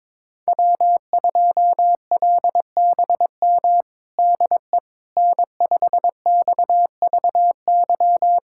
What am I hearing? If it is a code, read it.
Morse code, heard as a keyed tone, about 22 wpm, 700 Hz. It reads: W2LBM DE N5XVY
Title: USA Callsign DE Callsign 22wpm